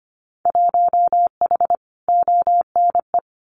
Morse code, heard as a keyed tone, about 25 wpm, 690 Hz. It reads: 1H ONE
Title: Winter Field Day Exchanges Encore 25wpm